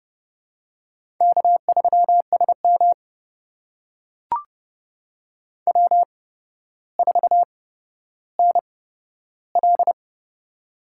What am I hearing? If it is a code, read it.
Morse code, heard as a keyed tone, about 32 wpm, spaced as for 13 wpm, 700 Hz. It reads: K3SM W4NL
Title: USA Call Signs - 8x Character Spacing - 30wpm